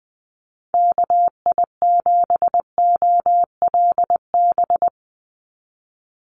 Morse code, heard as a keyed tone, about 19 wpm, 700 Hz. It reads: KI7OLB
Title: USA Call Signs 20wpm